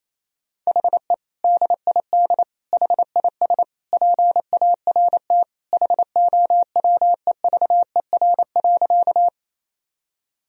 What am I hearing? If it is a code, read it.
Morse code, heard as a keyed tone, about 28 wpm, 700 Hz. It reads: HE DID HIS PART HOWEVER.
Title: Sentences from Top 1000 Words - Review Part 1 - 28wpm